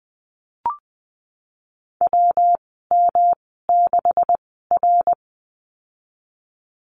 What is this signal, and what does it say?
Morse code, heard as a keyed tone, about 20 wpm, 700 Hz. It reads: W M 6 R
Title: 154 Lesson 25g - Letter p - Call Signs 20wpm